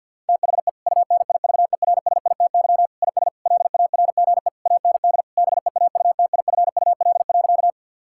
Morse code, heard as a keyed tone, about 48 wpm, 700 Hz. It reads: THE UNIVERSITY IS LARGE AND BEAUTIFUL.
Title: Sentences from Top 1000 Words - Review Part 5 - 50wpm